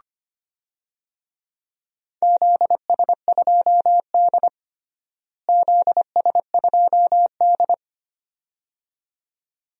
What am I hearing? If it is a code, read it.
Morse code, heard as a keyed tone, about 25 wpm, 700 Hz. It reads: ZS2D ZS2D
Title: DXCC-100 Call Signs - 4 Characters - Repeated 2x - 25wpm